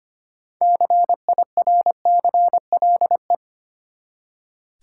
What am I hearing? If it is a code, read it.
Morse code, heard as a keyed tone, about 26 wpm, 700 Hz. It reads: CIRCLE
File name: practice / 25wpm / Easy Words with Hints - 25wpm